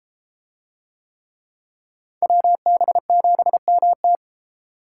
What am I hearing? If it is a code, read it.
Morse code, heard as a keyed tone, about 32 wpm, 700 Hz. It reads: WB7MT